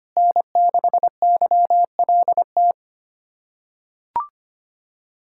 Morse code, heard as a keyed tone, about 25 wpm, 700 Hz. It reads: N6YLT
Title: USA Callsign DE Callsign - 5 Characters - 25wpm